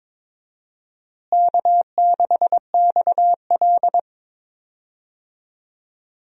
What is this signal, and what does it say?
Morse code, heard as a keyed tone, about 22 wpm, 700 Hz. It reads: K6XL
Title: USA Call Signs - 4 Characters - Repeated 2x - 22wpm